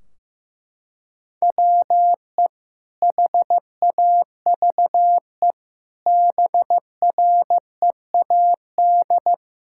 Morse code, heard as a keyed tone, about 15 wpm, 700 Hz. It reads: WE HAVE BREAD
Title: Sentences from Top 1600 Words 15wpm